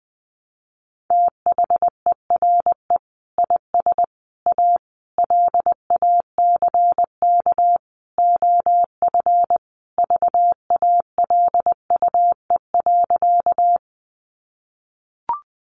Morse code, heard as a keyed tone, about 20 wpm, 700 Hz. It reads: THERE IS A LACK OF VALUE.
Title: Sentences from Top 1100 Words 20wpm